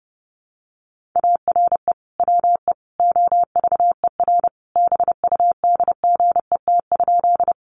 Morse code, heard as a keyed tone, about 30 wpm, 700 Hz. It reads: ARE WE OVER BUDGET?
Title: Sentences from Top 1200 Words 30wpm